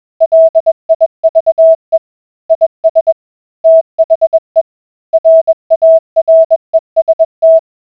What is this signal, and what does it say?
Morse code, heard as a keyed tone, about 21 wpm, 650 Hz. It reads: LIVE IS THE RAREST